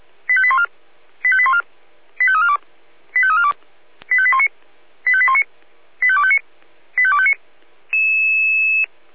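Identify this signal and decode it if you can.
Five-tone selective call (ZVEI), heard as quick runs of five tones, tones of about 70 ms. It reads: 85814 85814 85342 85342 85819 85819 85259 85259
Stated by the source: ZVEI five-tone selective call received with an SDR.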